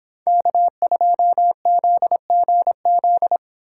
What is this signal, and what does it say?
Morse code, heard as a keyed tone, about 30 wpm, 700 Hz. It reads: K2ZGZ